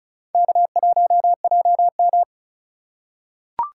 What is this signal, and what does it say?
Morse code, heard as a keyed tone, about 36 wpm, 700 Hz. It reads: K1JM
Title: USA Call Signs - 3x Character Spacing - 35wpm